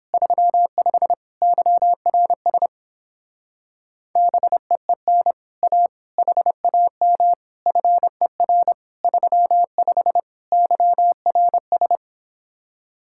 Morse code, heard as a keyed tone, about 26 wpm, 700 Hz. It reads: 35 YRS BEEN A HAM FER 35 YRS
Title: Speed-Racing Common QSO Phrases Encore - 22wpm